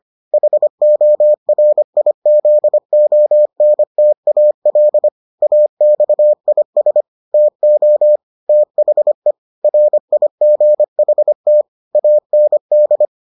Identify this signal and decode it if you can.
Morse code, heard as a keyed tone, about 25 wpm, 600 Hz. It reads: HORIZONTAL AXIS TO THE RIGHT AND